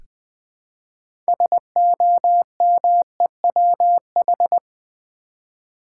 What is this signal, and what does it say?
Morse code, heard as a keyed tone, about 21 wpm, 700 Hz. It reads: SOMEWH
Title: Build A Word - 3 Letters At A Time - 8-Letter Words - 20wpm